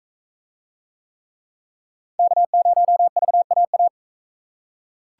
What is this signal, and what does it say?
Morse code, heard as a keyed tone, about 44 wpm, 700 Hz. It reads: K0VAA